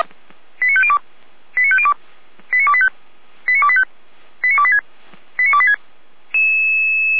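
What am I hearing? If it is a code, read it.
Five-tone selective call (ZVEI), heard as quick runs of five tones, tones of about 70 ms. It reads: 89502 89502 89285 89285 89286 89286
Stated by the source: ZVEI five-tone call 89270 received on an SDR.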